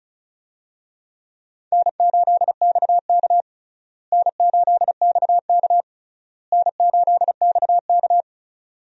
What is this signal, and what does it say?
Morse code, heard as a keyed tone, about 34 wpm, 700 Hz. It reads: N8XK N8XK N8XK
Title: USA Call Signs - 4 Characters - Repeated 3x - 35wpm